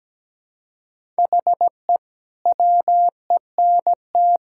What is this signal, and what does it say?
Morse code, heard as a keyed tone, about 17 wpm, 700 Hz. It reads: HE WENT
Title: Sentences from Top 1000 Words - Review Part 5 - 17wpm